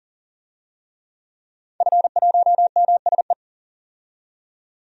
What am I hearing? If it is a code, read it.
Morse code, heard as a keyed tone, about 40 wpm, 700 Hz. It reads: F1MSE
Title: DXCC-100 Call Signs - 5 Characters - 40wpm